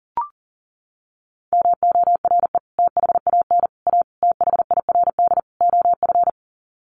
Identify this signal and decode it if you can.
Morse code, heard as a keyed tone, about 40 wpm, 700 Hz. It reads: MORE THAN A THIRD OF
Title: Sets of 5 Words Encore2 40wpm